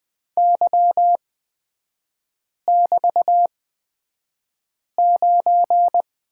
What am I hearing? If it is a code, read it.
Morse code, heard as a keyed tone, about 20 wpm, spaced as for 5 wpm, 700 Hz. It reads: Y<BT>9
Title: 224 Lesson 36e - BT - Three Characters 20wpm